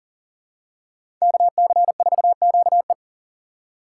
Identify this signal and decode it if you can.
Morse code, heard as a keyed tone, about 40 wpm, 700 Hz. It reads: KC4QE
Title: USA Call Signs - 6x Character Spacing - 40wpm